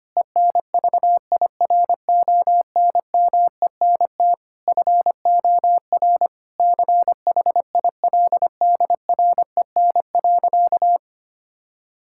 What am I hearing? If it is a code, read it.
Morse code, heard as a keyed tone, about 25 wpm, 700 Hz. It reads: ENVIRONMENT FOR CHILDREN.
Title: Sentences from Top 1000 Words - Review Part 5 - 25wpm